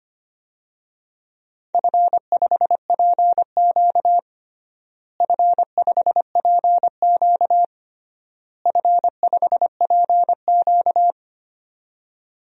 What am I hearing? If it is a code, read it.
Morse code, heard as a keyed tone, about 25 wpm, 700 Hz. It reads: F5PQ F5PQ F5PQ
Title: DXCC-100 Call Signs - 4 Characters - Repeated 3x - 25wpm